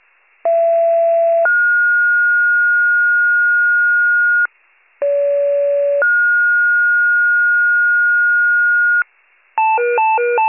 Two-tone page, heard as a one-second tone, then a three-second tone, about 670 then 1430 Hz, about 570 then 1430 Hz. It is heard from 0.4 to 9.0 s.